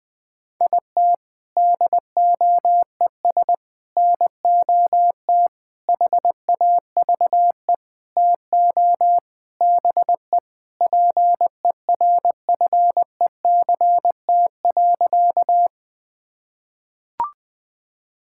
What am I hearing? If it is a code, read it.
Morse code, heard as a keyed tone, about 19 wpm, 700 Hz. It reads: IT DOES NOT HAVE TO BE PERFECT.
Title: Sentences with 5-Word Phrases - 20wpm